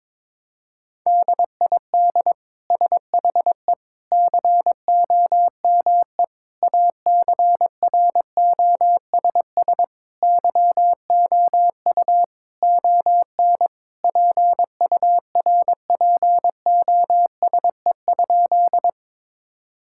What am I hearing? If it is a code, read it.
Morse code, heard as a keyed tone, about 22 wpm, 700 Hz. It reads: DID SHE COME ACROSS YOU ON PURPOSE?
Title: Sentences from Top 1000 Words - Review Part 5 - 22wpm